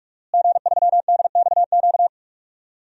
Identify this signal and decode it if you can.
Morse code, heard as a keyed tone, about 42 wpm, 700 Hz. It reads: G3DXQ
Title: DXCC-100 Call Signs - 5 Characters - Repeated 3x - 45wpm